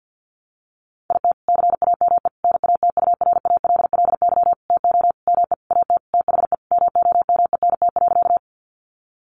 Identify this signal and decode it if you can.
Morse code, heard as a keyed tone, about 50 wpm, 700 Hz. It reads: IT CAME NATURALLY TO ME AT THE MOMENT.
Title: Sentences from Top 1300 Words 50wpm